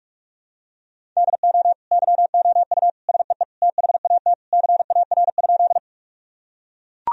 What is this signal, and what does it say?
Morse code, heard as a keyed tone, about 44 wpm, 700 Hz. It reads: DO YOU SEE THAT CAR?
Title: Sentences from Top 1000 Words - Review Part 1 - 45wpm